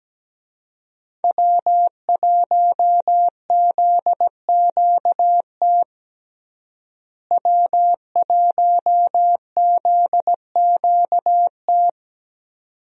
Morse code, heard as a keyed tone, about 17 wpm, 700 Hz. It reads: W1ZQT W1ZQT
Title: USA Call Signs - 5 Characters - Repeated 2x - 17wpm